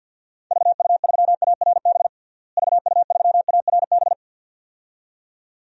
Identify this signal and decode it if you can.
Morse code, heard as a keyed tone, about 45 wpm, 700 Hz. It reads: VU3ARB VU3ARB
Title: DXCC-100 Call Signs - 6 Characters - Repeated 2x - 50wpm